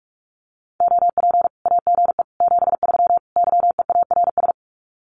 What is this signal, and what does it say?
Morse code, heard as a keyed tone, about 44 wpm, 700 Hz. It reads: OP AGE 73 YEARS
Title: Common QSO Phrases 45wpm